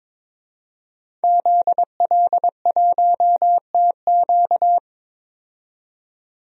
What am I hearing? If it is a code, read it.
Morse code, heard as a keyed tone, about 22 wpm, 700 Hz. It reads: ZL1TQ
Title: DXCC-100 Call Signs - 5 Characters - 22wpm